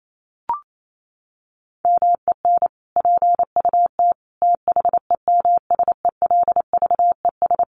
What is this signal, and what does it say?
Morse code, heard as a keyed tone, about 28 wpm, 700 Hz. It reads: MEN PUT THEMSELVES
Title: Sentences from Top 1400 Words 28wpm